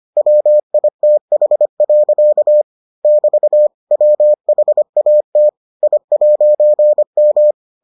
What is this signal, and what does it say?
Morse code, heard as a keyed tone, about 25 wpm, 600 Hz. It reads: WITH. <BT> WHAT I'M